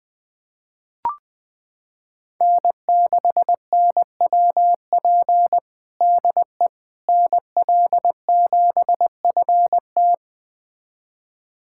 Morse code, heard as a keyed tone, about 20 wpm, 700 Hz. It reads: N6NWP DE NL7FT
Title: USA Callsign DE Callsign - 5 Characters - 20wpm